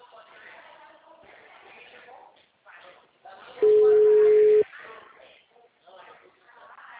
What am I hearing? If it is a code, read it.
Ringback tone (Europe, 425 Hz), with speech.